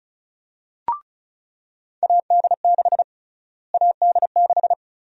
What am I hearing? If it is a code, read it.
Morse code, heard as a keyed tone, about 36 wpm, 700 Hz. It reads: AD6 AD6
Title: USA Call Sign Prefixes - Repeated 2x - 35wpm